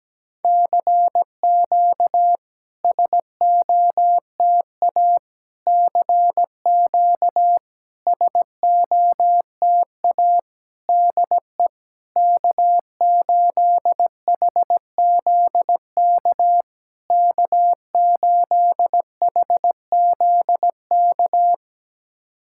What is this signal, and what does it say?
Morse code, heard as a keyed tone, about 17 wpm, 700 Hz. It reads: CQ SOTA CQ SOTA DE K8HZK K8HZK
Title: SOTA QSO Practice 17wpm